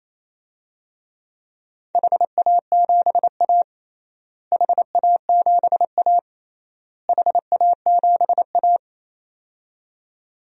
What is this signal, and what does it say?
Morse code, heard as a keyed tone, about 28 wpm, 700 Hz. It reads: HA7A HA7A HA7A
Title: DXCC-100 Call Signs - 4 Characters - Repeated 3x - 28wpm